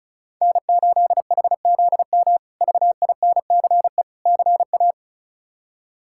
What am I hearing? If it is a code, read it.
Morse code, heard as a keyed tone, about 34 wpm, 700 Hz. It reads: N8HZM VINCE CA
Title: K1USN SST QSO Practice - 35wpm